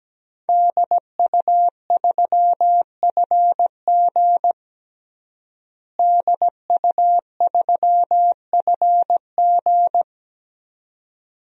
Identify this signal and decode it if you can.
Morse code, heard as a keyed tone, about 17 wpm, 700 Hz. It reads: DU3FG DU3FG
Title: DXCC-100 Call Signs - 5 Characters - Repeated 2x - 17wpm